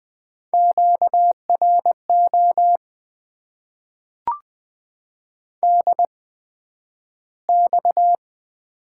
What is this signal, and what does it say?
Morse code, heard as a keyed tone, about 20 wpm, spaced as for 11 wpm, 700 Hz. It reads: QRO DX
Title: Top 90 QSO Elements - 8x Character Spacing - 20wpm